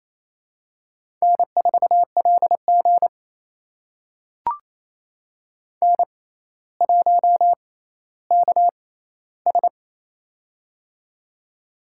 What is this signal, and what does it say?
Morse code, heard as a keyed tone, about 28 wpm, spaced as for 14 wpm, 700 Hz. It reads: N4LG N1KS
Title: USA Call Signs - 4 Characters - 6x Character Spacing - 28wpm